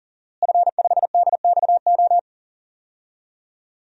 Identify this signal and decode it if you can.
Morse code, heard as a keyed tone, about 40 wpm, 700 Hz. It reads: F5DXO
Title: DXCC-100 Call Signs - 5 Characters - Repeated 3x - 40wpm